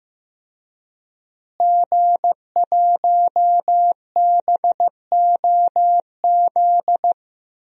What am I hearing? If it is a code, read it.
Morse code, heard as a keyed tone, about 15 wpm, 700 Hz. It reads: G1BOZ
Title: DXCC-100 Call Signs - 5 Characters - Repeated 3x - 15wpm